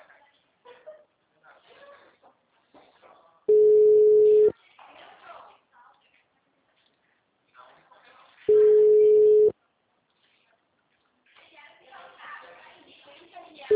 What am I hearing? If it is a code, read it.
Ringback tone (Europe, 425 Hz), with speech.
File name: out- -2017